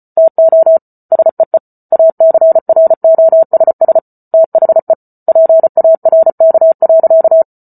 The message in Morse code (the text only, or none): TO SEE ACROSS THE PARK.